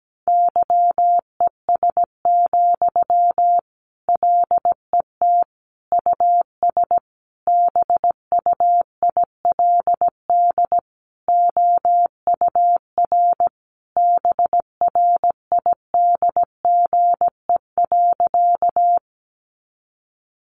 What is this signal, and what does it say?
Morse code, heard as a keyed tone, about 17 wpm, 700 Hz. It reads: YES, LET US BUILD OUR BRIDGE.
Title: Sentences from Top 1200 Words 17wpm